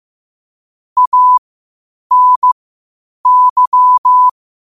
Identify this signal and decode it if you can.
Morse code, heard as a keyed tone, about 15 wpm, spaced as for 8 wpm, 1000 Hz. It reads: ANY
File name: 8wpm